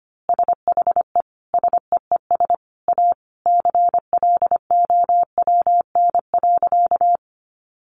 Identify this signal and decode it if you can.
Morse code, heard as a keyed tone, about 25 wpm, 700 Hz. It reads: SHE SEES A CLOWN.
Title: Easy Phrases 25wpm